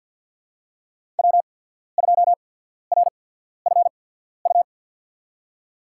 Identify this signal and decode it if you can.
Morse code, heard as a keyed tone, about 53 wpm, spaced as for 14 wpm, 700 Hz. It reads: W2RFU
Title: USA Call Signs - 5 Characters - 8x Character Spacing - 50wpm